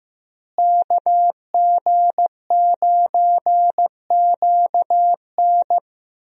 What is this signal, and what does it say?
Morse code, heard as a keyed tone, about 16 wpm, 700 Hz. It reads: KG9QN
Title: USA Call Signs - 5 Characters - Repeated 2x - 15wpm